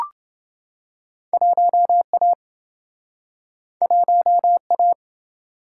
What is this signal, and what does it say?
Morse code, heard as a keyed tone, about 28 wpm, 700 Hz. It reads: 1A 1A